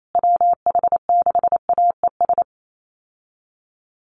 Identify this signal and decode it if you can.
Morse code, heard as a keyed tone, about 28 wpm, 700 Hz. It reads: WH6AES
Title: USA Call Signs Repeated 3x 28wpm